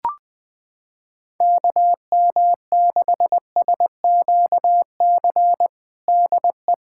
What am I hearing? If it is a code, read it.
Morse code, heard as a keyed tone, about 20 wpm, 700 Hz. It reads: KM6SQC DE
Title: USA Callsign DE Callsign - 6 Characters - 20wpm